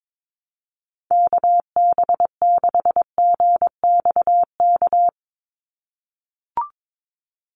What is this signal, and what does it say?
Morse code, heard as a keyed tone, about 22 wpm, 700 Hz. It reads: KB6GXK
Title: USA Call Signs 22wpm